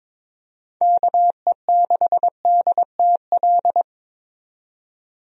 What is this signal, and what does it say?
Morse code, heard as a keyed tone, about 22 wpm, 700 Hz. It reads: KE6DTL
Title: USA Call Signs - 6 Characters - 22wpm